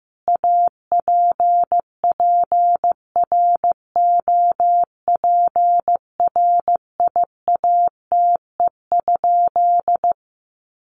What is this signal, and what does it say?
Morse code, heard as a keyed tone, about 15 wpm, 700 Hz. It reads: APPROPRIATE?
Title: Sentences from Top 900 Words 15wpm